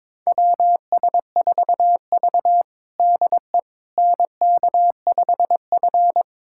Morse code, heard as a keyed tone, about 22 wpm, 700 Hz. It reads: WS4V DE NK5F
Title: USA Callsign DE Callsign - 4 Characters - 22wpm